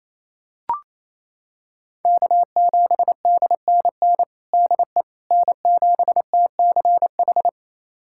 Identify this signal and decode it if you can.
Morse code, heard as a keyed tone, about 28 wpm, 700 Hz. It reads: K7DNN DE N7TCH
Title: USA Callsign DE Callsign - 5 Characters - 28wpm